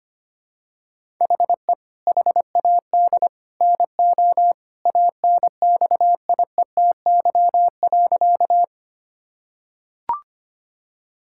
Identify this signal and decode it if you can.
Morse code, heard as a keyed tone, about 25 wpm, 700 Hz. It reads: HE HAD NO ANXIETY.
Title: Sentences from Top 1600 Words 25wpm